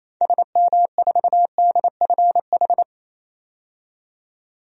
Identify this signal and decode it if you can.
Morse code, heard as a keyed tone, about 28 wpm, 700 Hz. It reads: SM4DFH
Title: DXCC-100 Call Signs - 6 Characters - 28wpm